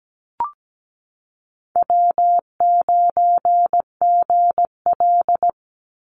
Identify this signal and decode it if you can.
Morse code, heard as a keyed tone, about 17 wpm, 700 Hz. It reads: W9GL
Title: Easy Call Signs Repeated 3x - 17wpm